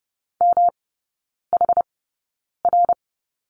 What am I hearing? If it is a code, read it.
Morse code, heard as a keyed tone, about 30 wpm, spaced as for 9 wpm, 700 Hz. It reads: MHR
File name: Lesson 18e - Letter m - Three Characters 30wpm